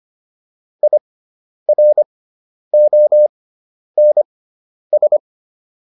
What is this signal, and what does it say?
Morse code, heard as a keyed tone, about 25 wpm, spaced as for 10 wpm, 600 Hz. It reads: IRONS